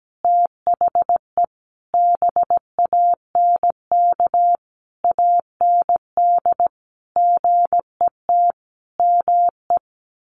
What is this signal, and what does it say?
Morse code, heard as a keyed tone, about 17 wpm, 700 Hz. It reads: THE BANK AND GET ME
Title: Sentences from Top 1500 Words 17wpm